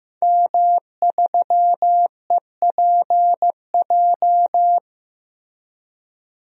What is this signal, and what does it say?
Morse code, heard as a keyed tone, about 15 wpm, 700 Hz. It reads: M3EPJ
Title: DXCC-100 Call Signs - 5 Characters - Repeated 2x - 15wpm